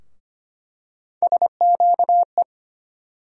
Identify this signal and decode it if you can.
Morse code, heard as a keyed tone, about 25 wpm, 700 Hz. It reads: SQE
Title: USA Call Sign Postfixes - 3 Letters 25wpm